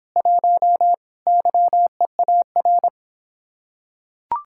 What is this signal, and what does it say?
Morse code, heard as a keyed tone, about 26 wpm, 700 Hz. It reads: 1 YEAR